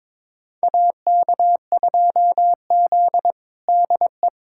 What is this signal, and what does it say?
Morse code, heard as a keyed tone, about 21 wpm, 700 Hz. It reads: AK2Z DE
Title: USA Callsign DE Callsign - 4 Characters - 22wpm